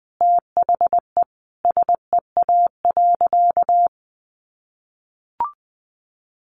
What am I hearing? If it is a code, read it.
Morse code, heard as a keyed tone, about 20 wpm, 700 Hz. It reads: THE SEA.
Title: Sentences from Top 1300 Words 20wpm